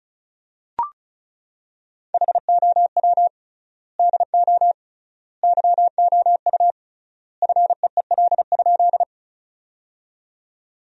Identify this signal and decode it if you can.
Morse code, heard as a keyed tone, about 36 wpm, 700 Hz. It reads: HOW DO YOU FEEL?
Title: 3x Word-Spacing - Sentences from Top 400 Words - 35wpm